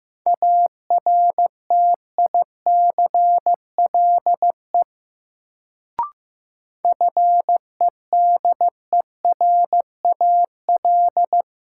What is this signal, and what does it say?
Morse code, heard as a keyed tone, about 15 wpm, 700 Hz. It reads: ARTICLE FEDERAL
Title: Sentences from Top 1000 Words - Review Part 5 - New Word Drill - 15wpm